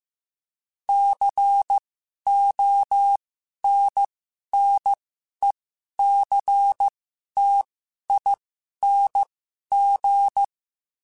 Morse code, heard as a keyed tone, about 15 wpm, spaced as for 11 wpm, 800 Hz. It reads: CONNECTING
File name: Special INGs-mp3s / Special INGs-mp3s / Sngl Ing @10